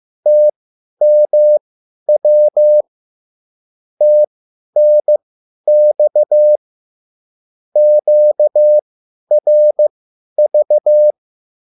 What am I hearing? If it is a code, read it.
Morse code, heard as a keyed tone, about 15 wpm, spaced as for 10 wpm, 600 Hz. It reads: TMW TNX QRV